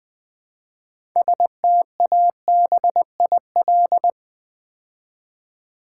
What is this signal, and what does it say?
Morse code, heard as a keyed tone, about 20 wpm, 700 Hz. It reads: STABIL
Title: Build A Word - 3 Letters At A Time - 9-Letter Words - 20wpm